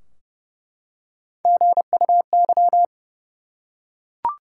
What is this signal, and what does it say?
Morse code, heard as a keyed tone, about 31 wpm, 700 Hz. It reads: GUY
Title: Three Letter Words - 30wpm